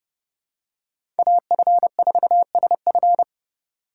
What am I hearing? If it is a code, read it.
Morse code, heard as a keyed tone, about 30 wpm, 700 Hz. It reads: AF4SF
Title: USA Call Signs - 5 Characters - 2x Character Spacing - 30wpm